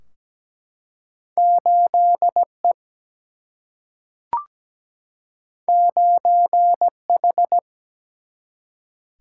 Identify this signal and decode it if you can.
Morse code, heard as a keyed tone, about 17 wpm, 700 Hz. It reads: 8 E 9 H
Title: Single-Letter Single-Number 17wpm